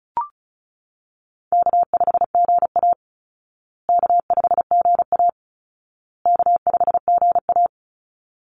Morse code, heard as a keyed tone, about 34 wpm, 700 Hz. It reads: K5GA K5GA K5GA
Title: Easy Call Signs Repeated 3x - 35wpm